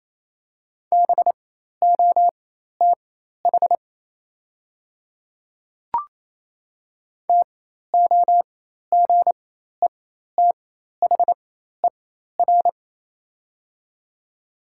Morse code, heard as a keyed tone, about 27 wpm, spaced as for 13 wpm, 700 Hz. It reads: BOTH TOGETHER
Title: 4x Character Spacing - Top 300 Words - 28wpm